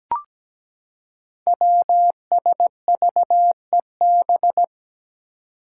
Morse code, heard as a keyed tone, about 17 wpm, 700 Hz. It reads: WSVEB
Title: Five Challenging Letter-Number Combinations 17wpm